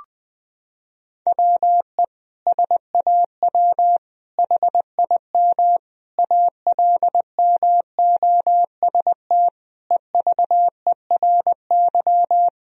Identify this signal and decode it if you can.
Morse code, heard as a keyed tone, about 19 wpm, 700 Hz. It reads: WE SAW HIM ALMOST EVERY
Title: Sentences from Top 1000 Words - Review Part 4 - 20wpm